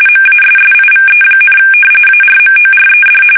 RTTY, about 45.45 baud. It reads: MASSIVE, ARMORED
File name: fsk850 / battleshipnj